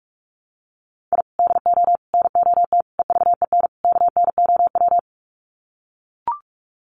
Morse code, heard as a keyed tone, about 44 wpm, 700 Hz. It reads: I DO NOT EVEN KNOW
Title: Sets of 5 Words 45wpm